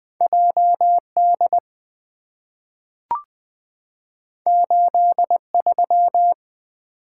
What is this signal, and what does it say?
Morse code, heard as a keyed tone, about 20 wpm, 700 Hz. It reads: J D 8 3
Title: Two Letter-Number Combinations - 20wpm